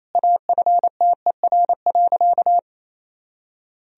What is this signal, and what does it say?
Morse code, heard as a keyed tone, about 28 wpm, 700 Hz. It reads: AFTER.
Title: Sentences from Top 500 Words and Abbreviations 28wpm